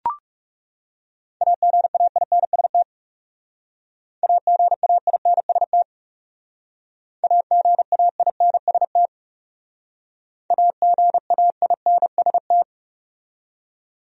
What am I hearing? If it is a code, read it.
Morse code, heard as a keyed tone, about 37 wpm, 700 Hz. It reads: AGAINST AGAINST AGAINST AGAINST
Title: Speed-Racing Top 500 Words 30wpm